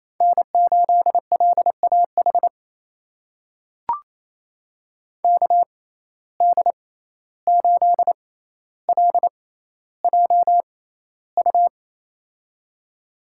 Morse code, heard as a keyed tone, about 27 wpm, spaced as for 14 wpm, 700 Hz. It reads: N8LAH KD8LJU
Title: USA Call Signs - 6x Character Spacing - 28wpm